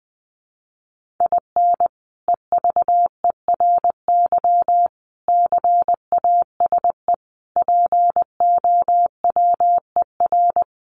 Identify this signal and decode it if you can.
Morse code, heard as a keyed tone, about 20 wpm, 700 Hz. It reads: IN EVERY CASE POWER
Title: Sentences from Top 1200 Words 20wpm